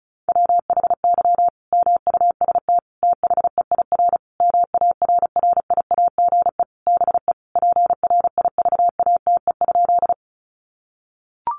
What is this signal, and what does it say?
Morse code, heard as a keyed tone, about 34 wpm, 700 Hz. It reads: WHY MUST THEIR MARRIAGE BE PRIVATE?
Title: Sentences from Top 1000 Words 35wpm